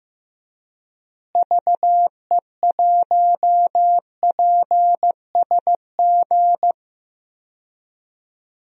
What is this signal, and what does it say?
Morse code, heard as a keyed tone, about 15 wpm, 700 Hz. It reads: VE1PSG
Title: DXCC-100 Call Signs - 6 Characters - Repeated 3x - 15wpm